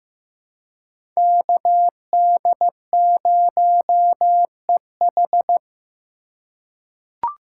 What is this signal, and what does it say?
Morse code, heard as a keyed tone, about 15 wpm, 700 Hz. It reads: KD0EH
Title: USA Call Signs - 4x Character Spacing - 15wpm